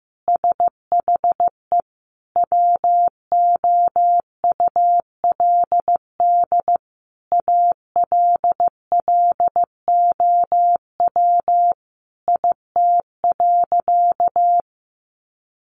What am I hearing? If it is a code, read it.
Morse code, heard as a keyed tone, about 14 wpm, 700 Hz. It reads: SHE WOULD ALLOW IT.
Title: Sentences from Top 700 Words 15wpm